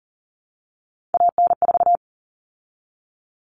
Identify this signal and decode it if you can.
Morse code, heard as a keyed tone, about 40 wpm, 700 Hz. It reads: AN4
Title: All Cut Numbers - 3-Digit 40wpm